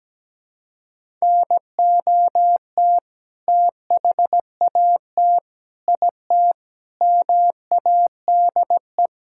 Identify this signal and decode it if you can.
Morse code, heard as a keyed tone, about 17 wpm, 700 Hz. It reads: NOT THAT IT MADE
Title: Sentences from Top 1000 Words - Review Part 5 - 17wpm